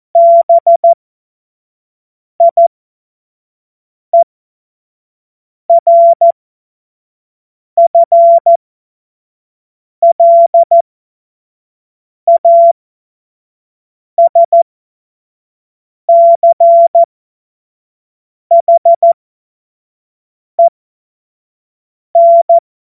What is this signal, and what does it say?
Morse code, heard as a keyed tone, about 14 wpm, spaced as for 5 wpm, 680 Hz. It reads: BIERFLASCHEN
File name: Morsekurs / Witze / Witz 08. August 2023 25 F70